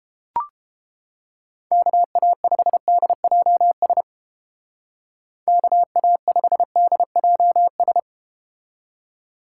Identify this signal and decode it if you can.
Morse code, heard as a keyed tone, about 32 wpm, 700 Hz. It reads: KA5DJS KA5DJS